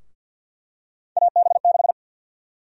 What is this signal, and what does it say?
Morse code, heard as a keyed tone, about 52 wpm, 700 Hz. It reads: AB6
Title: USA Call Sign Prefixes - Repeated 2x - 50wpm